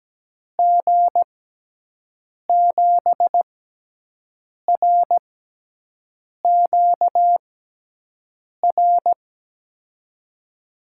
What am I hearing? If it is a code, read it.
Morse code, heard as a keyed tone, about 17 wpm, spaced as for 6 wpm, 700 Hz. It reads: G7RQR
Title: USA Call Signs - 6 Characters - 6x Character Spacing - 17wpm